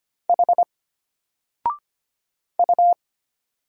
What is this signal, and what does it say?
Morse code, heard as a keyed tone, about 25 wpm, 700 Hz. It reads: H U
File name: Single-Letter-Number-25wpm